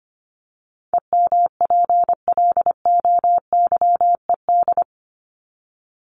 Morse code, heard as a keyed tone, about 25 wpm, 700 Hz. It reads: EMPLOYED
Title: ED Suffix - Top 100 Words with Base Words 25wpm